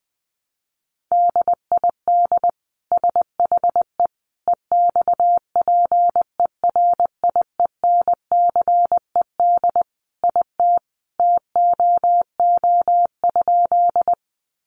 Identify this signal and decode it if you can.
Morse code, heard as a keyed tone, about 20 wpm, 700 Hz. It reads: DID SHE EXPERIENCED IT TOO?
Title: Sentences from Top 1100 Words 20wpm